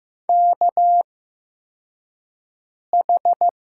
Morse code, heard as a keyed tone, about 15 wpm, spaced as for 4 wpm, 700 Hz. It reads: KH
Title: USA Call Signs - 8x Character Spacing - 15wpm